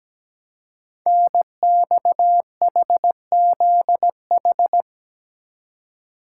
Morse code, heard as a keyed tone, about 17 wpm, 700 Hz. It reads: NXHZH
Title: Code Groups - 5 Letters - 17wpm